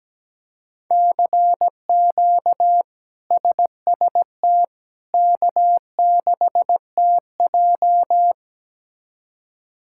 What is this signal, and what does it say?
Morse code, heard as a keyed tone, about 17 wpm, 700 Hz. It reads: CQ SST K6TJ
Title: K1USN SST QSO Practice - 17wpm